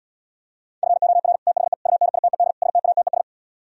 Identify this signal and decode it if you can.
Morse code, heard as a keyed tone, about 77 wpm, 700 Hz. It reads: FOR THE UNITED STATES